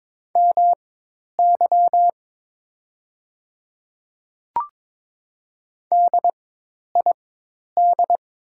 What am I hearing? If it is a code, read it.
Morse code, heard as a keyed tone, about 23 wpm, spaced as for 10 wpm, 700 Hz. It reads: MY DID
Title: 4x Character Spacing - Top 100 Words - 22wpm